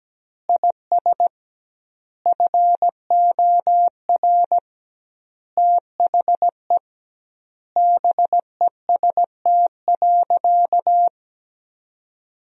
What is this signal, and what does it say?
Morse code, heard as a keyed tone, about 17 wpm, 700 Hz. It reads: IS FOR THE BEST.
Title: 2x Word-Spacing - Sentences from Top 500 Words - 17wpm